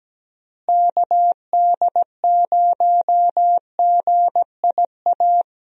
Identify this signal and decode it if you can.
Morse code, heard as a keyed tone, about 17 wpm, 700 Hz. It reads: KD0GIA
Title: USA Call Signs - 6 Characters - Repeated 3x - 17wpm